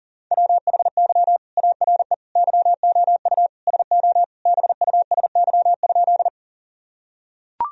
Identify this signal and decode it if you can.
Morse code, heard as a keyed tone, about 40 wpm, 700 Hz. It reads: WHY ARE YOU SO BUSY?
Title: Sentences from Top 1000 Words - Review Part 3 - 40wpm